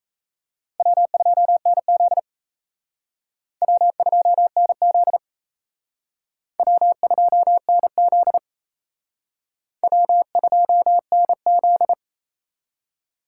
Morse code, heard as a keyed tone, about 34 wpm, 700 Hz. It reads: W2NZ W2NZ W2NZ W2NZ
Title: Speed Racing - USA Call Signs 28wpm